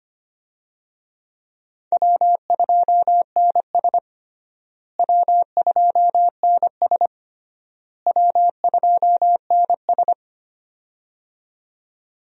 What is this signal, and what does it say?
Morse code, heard as a keyed tone, about 25 wpm, 700 Hz. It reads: W2NS W2NS W2NS
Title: USA Call Signs - 4 Characters - Repeated 3x - 25wpm